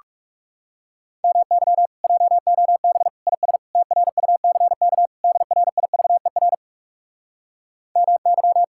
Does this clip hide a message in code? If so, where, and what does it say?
Yes: Morse code from 1.2 to 8.6 s, heard as a keyed tone, about 40 wpm, 700 Hz. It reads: MY JOB IS TRUCK DRIVER MY